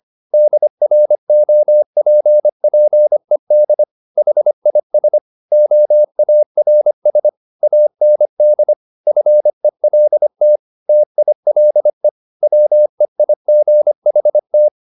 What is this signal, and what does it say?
Morse code, heard as a keyed tone, about 25 wpm, 600 Hz. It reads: DROPPED HIS OARS AND FELT TILE WEIGHT
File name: the old man and the sea part 1 25wpm